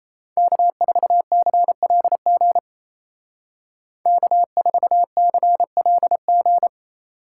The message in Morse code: K4CLG K4CLG

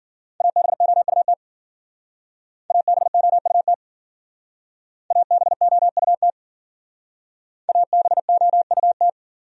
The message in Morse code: ABOUT ABOUT ABOUT ABOUT

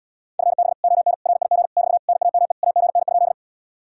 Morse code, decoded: FB ON RIG <BT> NAME AGN?

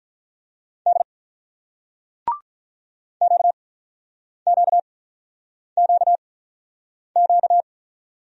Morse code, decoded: D QQQQ

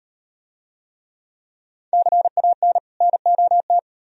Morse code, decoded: CAN NOT

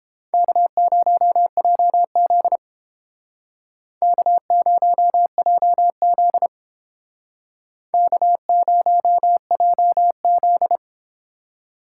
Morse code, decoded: K0JZ K0JZ K0JZ

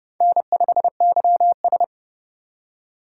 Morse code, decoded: N5YS

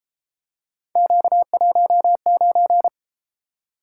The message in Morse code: Q19

